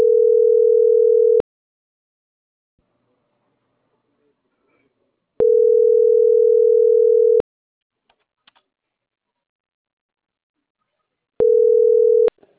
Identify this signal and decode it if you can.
Ringback tone (North America), 440 + 480 Hz.